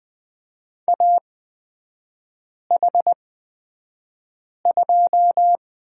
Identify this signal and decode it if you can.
Morse code, heard as a keyed tone, about 20 wpm, spaced as for 5 wpm, 700 Hz. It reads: AH2
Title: 236 Lesson 38e - Letter q - Three Characters 20wpm